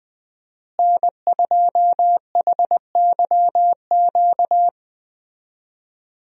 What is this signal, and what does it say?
Morse code, heard as a keyed tone, about 20 wpm, 700 Hz. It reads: N2HYQ
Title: USA Call Signs - 5 Characters - Repeated 2x - 20wpm